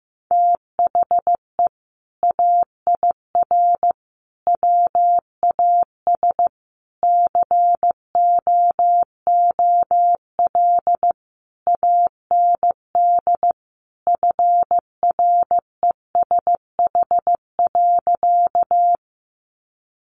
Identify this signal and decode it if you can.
Morse code, heard as a keyed tone, about 14 wpm, 700 Hz. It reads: THE AIR WAS COOL AND FRESH.
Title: Sentences from Top 1200 Words 15wpm